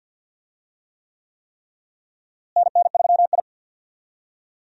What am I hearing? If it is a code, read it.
Morse code, heard as a keyed tone, about 52 wpm, 700 Hz. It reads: NN3I
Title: USA Call Signs - 4 Characters - Repeated 3x - 50wpm